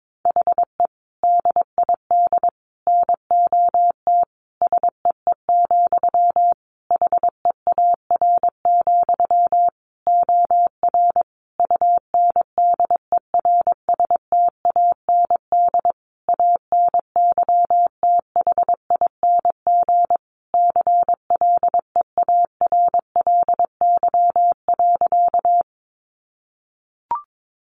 Morse code, heard as a keyed tone, about 22 wpm, 700 Hz. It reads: HE DID NOT SEE, HEAR, OR UNDERSTAND ANYTHING CLEARLY.
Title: Sentences from Top 1300 Words 22wpm